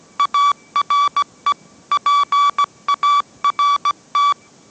Morse code, heard as a keyed tone, about 17 wpm, 1150 Hz. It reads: ARE PART
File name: mrx / mrx080 18wpm